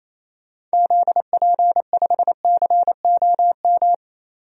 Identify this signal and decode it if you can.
Morse code, heard as a keyed tone, about 28 wpm, 700 Hz. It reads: ZP5COM
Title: DXCC-100 Call Signs - 6 Characters - 28wpm